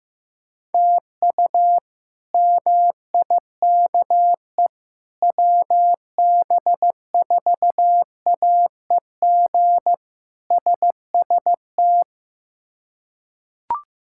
Morse code, heard as a keyed tone, about 15 wpm, 700 Hz. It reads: TU MIKE WB4AEG SST
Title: K1USN SST QSO Practice - 15wpm